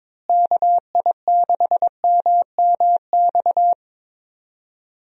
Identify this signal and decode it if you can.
Morse code, heard as a keyed tone, about 22 wpm, 700 Hz. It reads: KI6MMX
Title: USA Call Signs - 6 Characters - Repeated 3x - 22wpm